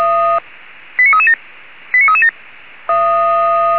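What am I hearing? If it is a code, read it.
Five-tone selective call (ZVEI), heard as quick runs of five tones, tones of about 70 ms. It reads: 89337 89337
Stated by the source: ZVEI five-tone selective call received by SDR.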